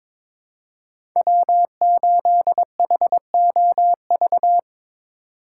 Morse code, heard as a keyed tone, about 22 wpm, 700 Hz. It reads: W8HOV
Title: USA Call Signs - 5 Characters - 6x Character Spacing - 22wpm